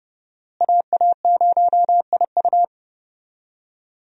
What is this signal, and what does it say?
Morse code, heard as a keyed tone, about 29 wpm, 700 Hz. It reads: AA0IU
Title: USA Call Signs - 5 Characters - Repeated 2x - 30wpm